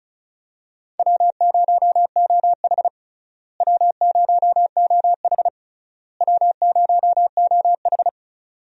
Morse code, heard as a keyed tone, about 34 wpm, 700 Hz. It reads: W0OH W0OH W0OH
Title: USA Call Signs - 4 Characters - Repeated 3x - 35wpm